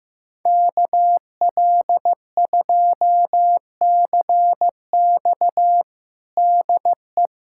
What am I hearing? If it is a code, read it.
Morse code, heard as a keyed tone, about 15 wpm, 700 Hz. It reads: KL2CX DE
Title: USA Callsign DE Callsign - 5 Characters - 15wpm